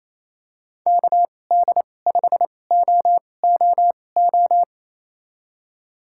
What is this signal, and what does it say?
Morse code, heard as a keyed tone, about 27 wpm, spaced as for 20 wpm, 700 Hz. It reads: KD5OOO
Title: USA Call Signs - 6 Characters - 2x Character Spacing - 28wpm